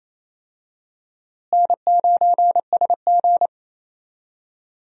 Morse code, heard as a keyed tone, about 28 wpm, 700 Hz. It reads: N9SG
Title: USA Call Signs - 4 Characters - Repeated 2x - 28wpm